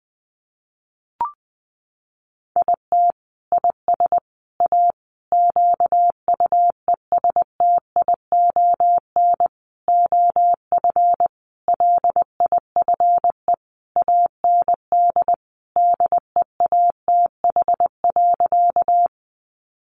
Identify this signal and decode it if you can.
Morse code, heard as a keyed tone, about 20 wpm, 700 Hz. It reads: IT IS A QUESTION OF LIFE AND DEATH.
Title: Sentences from Top 1100 Words 20wpm